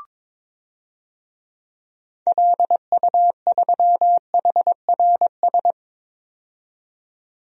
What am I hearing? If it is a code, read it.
Morse code, heard as a keyed tone, about 22 wpm, 700 Hz. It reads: LU3HRS
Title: DXCC-100 Call Signs - 6 Characters - 22wpm